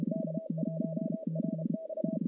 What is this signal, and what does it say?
Morse code, heard as a keyed tone, about 31 wpm, 640 Hz, with noise. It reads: K1RX
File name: contests / CW-SS-k5tr- -1200